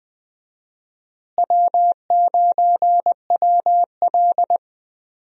Morse code, heard as a keyed tone, about 20 wpm, 700 Hz. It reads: W9WL
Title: USA Call Signs - 4 Characters - Repeated 3x - 20wpm